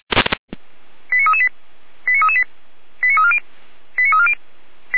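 Five-tone selective call (ZVEI), heard as quick runs of five tones, tones of about 70 ms. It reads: 89338 89338 89340 89340
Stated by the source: ZVEI five-tone selective call received by SDR.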